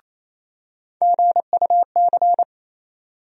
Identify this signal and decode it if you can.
Morse code, heard as a keyed tone, about 28 wpm, 700 Hz. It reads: GUC
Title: USA Call Sign Postfixes - 3 Letters 28wpm